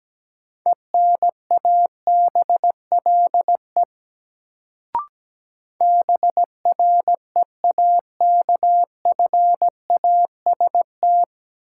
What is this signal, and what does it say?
Morse code, heard as a keyed tone, about 17 wpm, 700 Hz. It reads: ENABLE BREAKFAST
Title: Sentences from Top 1500 Words - New Word Drill Alt - 17wpm